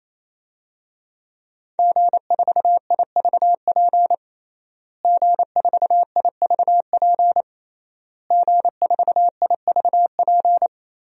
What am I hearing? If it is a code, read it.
Morse code, heard as a keyed tone, about 28 wpm, 700 Hz. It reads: G4IVP G4IVP G4IVP
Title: DXCC-100 Call Signs - 5 Characters - Repeated 3x - 28wpm